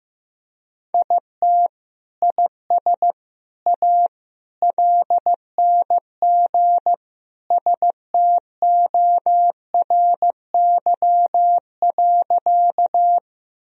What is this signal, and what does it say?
Morse code, heard as a keyed tone, about 16 wpm, 700 Hz. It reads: IT IS A LNG STORY.
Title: Sentences from Top 300 Words and Abbreviations 15wpm